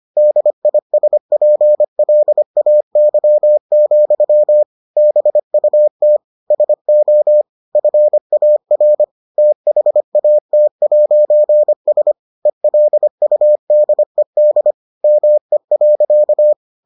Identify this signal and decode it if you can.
Morse code, heard as a keyed tone, about 25 wpm, 600 Hz. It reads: DISPLAY, BUT SO FAR THAT'S ELUDED ME.